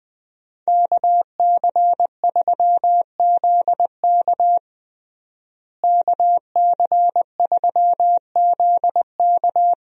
Morse code, heard as a keyed tone, about 20 wpm, 700 Hz. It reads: KC3ZK KC3ZK
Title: USA Call Signs - 5 Characters - Repeated 2x - 20wpm